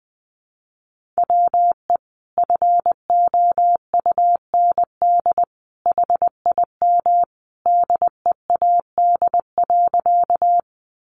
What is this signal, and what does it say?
Morse code, heard as a keyed tone, about 20 wpm, 700 Hz. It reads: WE FOUND HIM DEAD.
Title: Sentences from Top 1500 Words 20wpm